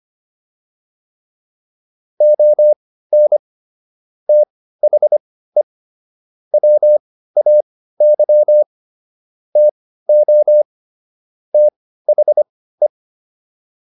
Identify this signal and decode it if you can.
Morse code, heard as a keyed tone, about 25 wpm, spaced as for 15 wpm, 600 Hz. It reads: ON THE WAY TO THE